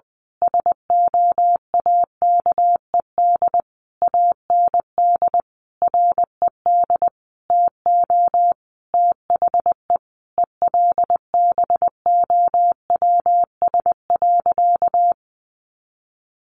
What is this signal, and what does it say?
Morse code, heard as a keyed tone, about 20 wpm, 700 Hz. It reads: SOAKED AND RED TO THE ELBOWS.
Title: A Sound Of Thunder 20wpm